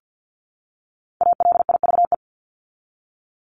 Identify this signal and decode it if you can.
Morse code, heard as a keyed tone, about 50 wpm, 700 Hz. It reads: ALIVE
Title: Top 1751-2000 Words 50wpm